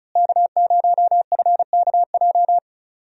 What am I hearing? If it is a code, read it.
Morse code, heard as a keyed tone, about 36 wpm, 700 Hz. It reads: K0FKJ
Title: USA Call Signs - 5 Characters - Repeated 2x - 35wpm